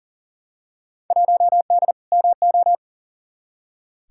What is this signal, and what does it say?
Morse code, heard as a keyed tone, about 40 wpm, 700 Hz. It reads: 1D MO
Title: Field Day Exchanges - 40wpm